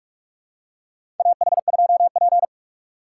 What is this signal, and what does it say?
Morse code, heard as a keyed tone, about 46 wpm, 700 Hz. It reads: AH2P
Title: USA Call Signs - 6x Character Spacing - 45wpm